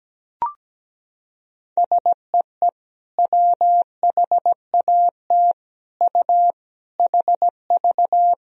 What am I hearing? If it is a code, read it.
Morse code, heard as a keyed tone, about 17 wpm, 700 Hz. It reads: SEE WHAT U HV
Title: Sentences from Top 500 Words and Abbreviations 17wpm